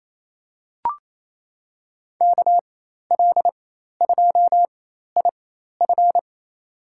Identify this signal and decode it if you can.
Morse code, heard as a keyed tone, about 28 wpm, spaced as for 13 wpm, 700 Hz. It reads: KL2IF
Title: USA Call Signs - 4x Character Spacing - 28wpm